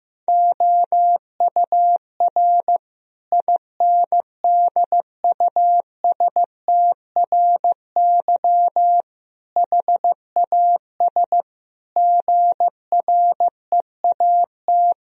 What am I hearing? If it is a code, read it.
Morse code, heard as a keyed tone, about 15 wpm, 700 Hz. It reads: OUR INDUSTRY HAS GREAT
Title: Sentences from Top 1000 Words - Review Part 4 - 15wpm